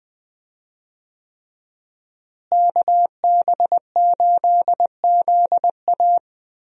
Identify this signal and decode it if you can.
Morse code, heard as a keyed tone, about 20 wpm, 700 Hz. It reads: KB8ZA
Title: USA Call Signs - 5 Characters - Repeated 3x - 20wpm